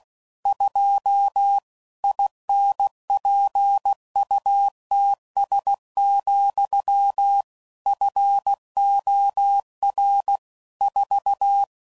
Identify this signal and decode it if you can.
Morse code, heard as a keyed tone, about 16 wpm, 800 Hz. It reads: 2 INPUTS, FOR 4
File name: CW Code Groups @15WPM / Technical Subjects / AND Gate